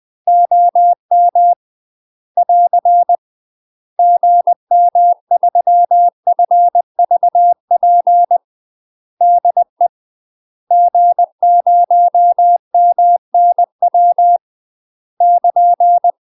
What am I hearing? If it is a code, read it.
Morse code, heard as a keyed tone, about 20 wpm, 700 Hz. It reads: OM <AR> GM3FVP DE G0MNW <KN>
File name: UK MASTER QSOs TRAINING / 20WPM / UK MASTER QSOs 0060